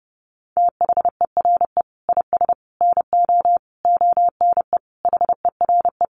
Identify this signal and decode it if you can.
Morse code, heard as a keyed tone, about 30 wpm, 700 Hz. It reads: THERE IS NO ONE HERE
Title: Sentences from Top 200 Words 30wpm